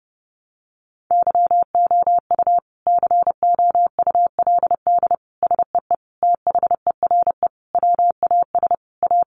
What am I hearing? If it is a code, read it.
Morse code, heard as a keyed tone, about 30 wpm, 700 Hz. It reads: YOU COULD SEE THERE WAS A
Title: Sentences from Top 500 Words 30wpm